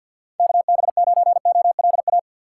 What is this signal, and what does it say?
Morse code, heard as a keyed tone, about 53 wpm, 690 Hz. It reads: KB9OLA
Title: USA Call Signs - 6 Characters - 8x Character Spacing - 50wpm